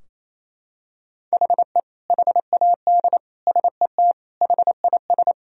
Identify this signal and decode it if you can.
Morse code, heard as a keyed tone, about 28 wpm, 700 Hz. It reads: HE HAD SET HIS
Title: Sentences from Top 1000 Words - Review Part 2 - 28wpm